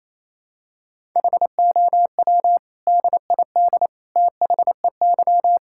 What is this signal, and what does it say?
Morse code, heard as a keyed tone, about 28 wpm, 700 Hz. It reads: HOW DID THEY
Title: Sentences from Top 1000 Words - Review Part 5 - 28wpm